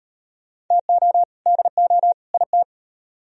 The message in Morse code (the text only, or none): TO DO IT